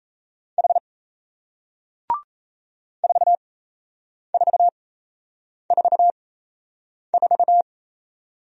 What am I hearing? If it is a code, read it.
Morse code, heard as a keyed tone, about 38 wpm, spaced as for 8 wpm, 700 Hz. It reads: H 4444